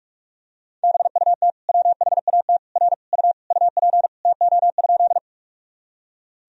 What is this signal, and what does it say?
Morse code, heard as a keyed tone, about 46 wpm, 700 Hz. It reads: BUT WHAT R U UP TO?
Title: Sentences from Top 100 Words and Abbreviations 45wpm